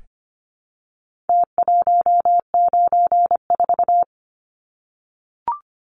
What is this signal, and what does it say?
Morse code, heard as a keyed tone, about 25 wpm, 700 Hz. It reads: T194
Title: Numbers - Contest Style - 4-Digit 25wpm